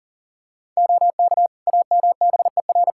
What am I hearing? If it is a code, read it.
Morse code, heard as a keyed tone, about 40 wpm, 700 Hz. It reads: OK AMBER
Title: Practice QSO - Exchange 3 - More Info - 40wpm